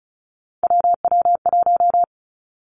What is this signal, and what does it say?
Morse code, heard as a keyed tone, about 36 wpm, 700 Hz. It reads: WW1
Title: Three Challenging Letter-Number Combinations 35wpm